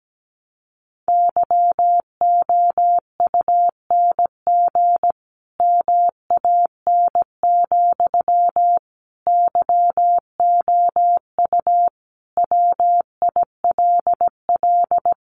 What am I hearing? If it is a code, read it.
Morse code, heard as a keyed tone, about 17 wpm, 700 Hz. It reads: YOUNG MAN, YOU WILL
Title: Sentences from Top 400 Words 17wpm